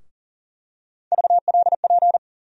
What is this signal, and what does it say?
Morse code, heard as a keyed tone, about 40 wpm, 700 Hz. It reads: VLP
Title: USA Call Sign Postfixes - 3 Letters 40wpm